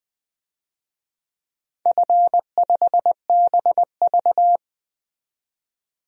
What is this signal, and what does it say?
Morse code, heard as a keyed tone, about 20 wpm, 700 Hz. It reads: F5BV
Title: DXCC-100 Call Signs - 4 Characters - 20wpm